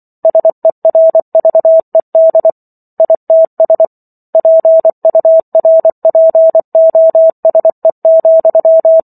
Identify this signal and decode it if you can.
Morse code, heard as a keyed tone, about 24 wpm, 650 Hz. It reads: SERVED ITS PURPOSE,